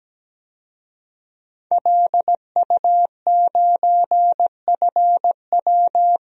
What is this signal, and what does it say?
Morse code, heard as a keyed tone, about 17 wpm, 700 Hz. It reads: LU9FW
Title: DXCC-100 Call Signs - 5 Characters - 17wpm